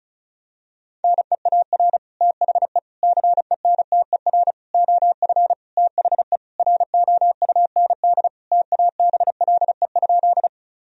Morse code, heard as a keyed tone, about 34 wpm, 700 Hz. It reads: NEAR THE CENTER OF THE ROUND TABLE?
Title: Sentences from Top 1000 Words - Review Part 2 - 35wpm